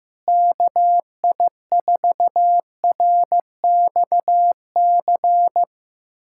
Morse code, heard as a keyed tone, about 15 wpm, 700 Hz. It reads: KI4RXC
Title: USA Callsign DE Callsign 15wpm